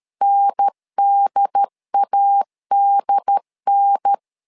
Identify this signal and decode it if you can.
Morse code, heard as a keyed tone, about 13 wpm, 800 Hz. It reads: NDADN
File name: LEZIONI / 3 MWRO / 3 MW 12wpm 1633